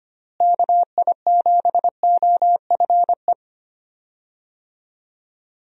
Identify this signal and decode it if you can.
Morse code, heard as a keyed tone, about 25 wpm, 700 Hz. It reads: KI7OFE
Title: USA Call Signs - 6 Characters - Repeated 3x - 25wpm